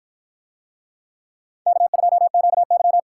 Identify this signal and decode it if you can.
Morse code, heard as a keyed tone, about 53 wpm, 700 Hz. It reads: K2QY